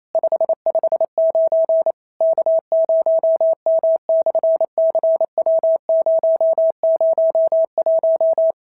Morse code, heard as a keyed tone, about 28 wpm, 650 Hz. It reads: 559 K0M/CW001
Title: SOTA QSO Practice 28wpm